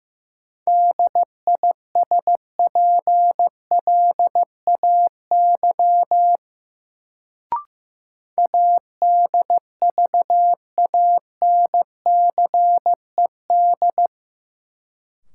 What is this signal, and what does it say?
Morse code, heard as a keyed tone, about 15 wpm, 700 Hz. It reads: DISPLAY ADVANCED
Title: Sentences from Top 1200 Words - New Word Drill - 15wpm